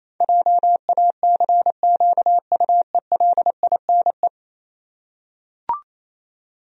Morse code, heard as a keyed tone, about 28 wpm, 700 Hz. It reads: JACQUELINE
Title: USA Common Names - 28wpm